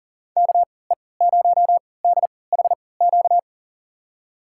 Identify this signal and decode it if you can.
Morse code, heard as a keyed tone, about 39 wpm, spaced as for 22 wpm, 700 Hz. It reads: KE0DHQ
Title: USA Call Signs - 6 Characters - 3x Character Spacing - 40wpm